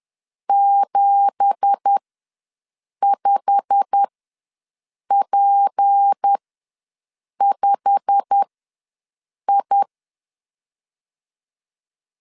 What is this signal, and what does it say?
Morse code, heard as a keyed tone, about 10 wpm, spaced as for 6 wpm, 800 Hz. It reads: 75P5I
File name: LEZIONI / 10 SIGNS / 10D COMA ws 10wpm 4314